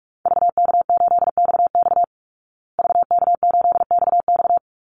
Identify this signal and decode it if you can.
Morse code, heard as a keyed tone, about 44 wpm, 700 Hz. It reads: VK8XX VK8XX
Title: Call Signs Repeated 2x - 45wpm